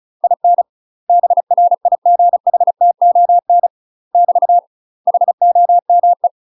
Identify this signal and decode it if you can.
Morse code, heard as a keyed tone, about 34 wpm, 700 Hz. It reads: IN BRIGHTON <BT> HOME
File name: UK MASTER QSOs TRAINING / 35WPM / UK MASTER QSOs 0008